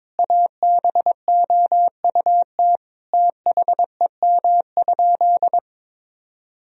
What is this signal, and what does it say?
Morse code, heard as a keyed tone, about 22 wpm, 700 Hz. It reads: ABOUT THEM?
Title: Sentences from Top 1000 Words - Review Part 3 - 22wpm